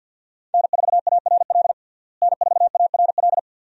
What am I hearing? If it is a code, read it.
Morse code, heard as a keyed tone, about 50 wpm, 700 Hz. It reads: N4ARL N4ARL
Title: USA Call Signs - 5 Characters - Repeated 2x - 50wpm